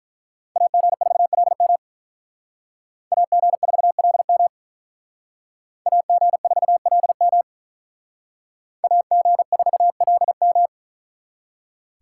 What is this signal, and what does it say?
Morse code, heard as a keyed tone, about 45 wpm, 700 Hz. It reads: AG4LM AG4LM AG4LM AG4LM